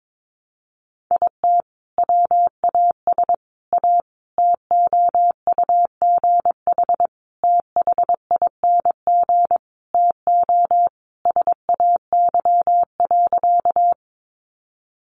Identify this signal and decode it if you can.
Morse code, heard as a keyed tone, about 22 wpm, 700 Hz. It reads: IT WAS A TOUGH THING TO SAY.
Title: Sentences from Top 1200 Words 22wpm